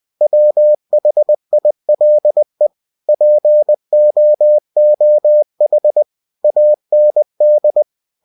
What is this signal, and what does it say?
Morse code, heard as a keyed tone, about 20 wpm, 600 Hz. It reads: WHILE POOH AND
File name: cw-books / winnie-the-pooh / winnie-20wpm-page-0095